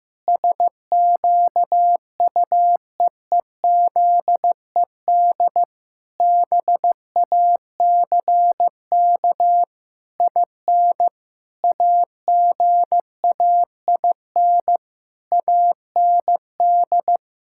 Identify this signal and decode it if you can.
Morse code, heard as a keyed tone, about 16 wpm, 700 Hz. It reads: SQUEEZED BACK IN AGAIN AND
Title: Pangram Challenge v2 - 15wpm